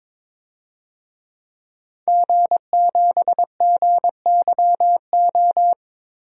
Morse code, heard as a keyed tone, about 22 wpm, 700 Hz. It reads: G7GYO
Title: DXCC-100 Call Signs - 5 Characters - Repeated 2x - 22wpm